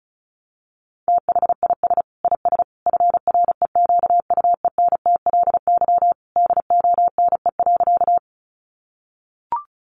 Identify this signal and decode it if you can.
Morse code, heard as a keyed tone, about 34 wpm, 700 Hz. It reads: THIS IS FREQUENTLY DONE.
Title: Sentences from Top 1300 Words 35wpm